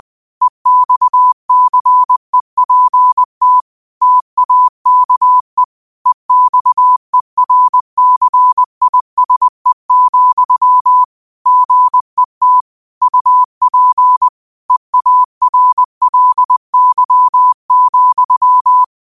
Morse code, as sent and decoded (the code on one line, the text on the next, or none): . -..- -.-. .--. - / - .- -.- . / . -..- . .-. -.-. .. ... --..-- / --. - / ..- .--. / . .- .-. .-.. -.-- --..--
EXCEPT TAKE EXERCISE, GET UP EARLY,